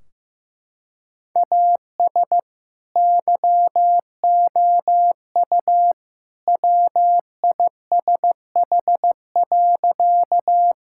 Morse code, heard as a keyed tone, about 15 wpm, 700 Hz. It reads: AS YOU WISH.
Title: Sentences from Top 1600 Words 15wpm